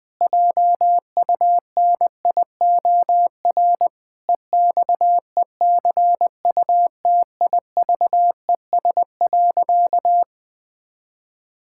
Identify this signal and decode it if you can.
Morse code, heard as a keyed tone, about 20 wpm, 700 Hz. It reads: JUNIOR EXECUTIVES.
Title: Pangram Challenge v2 - 20wpm